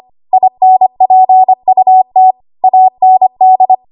Morse code, heard as a keyed tone, about 25 wpm, 750 Hz. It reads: INPUT AND
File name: Archive / 25 WPM / 25WPM